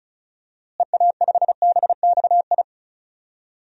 Morse code, heard as a keyed tone, about 36 wpm, 700 Hz. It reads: EA5BXI
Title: DXCC-100 Call Signs - 6 Characters - Repeated 3x - 35wpm